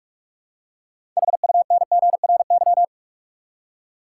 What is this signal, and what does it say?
Morse code, heard as a keyed tone, about 46 wpm, 700 Hz. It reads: HUNGRY
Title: Six Letter Words - 45wpm